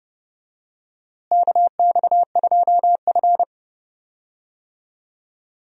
Morse code, heard as a keyed tone, about 30 wpm, 700 Hz. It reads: KX2F
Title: USA Call Signs - 4 Characters - Repeated 2x - 30wpm